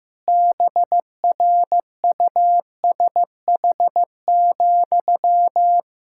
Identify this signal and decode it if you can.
Morse code, heard as a keyed tone, about 15 wpm, 700 Hz. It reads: BRUSH,
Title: Pangram Challenge v2 - 15wpm